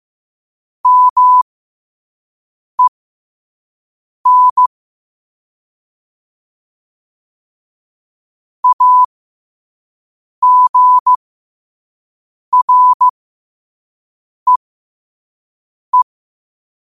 Morse code, heard as a keyed tone, about 15 wpm, spaced as for 5 wpm, 1000 Hz. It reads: MEN AGREE
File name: qotdfiles / 5wpm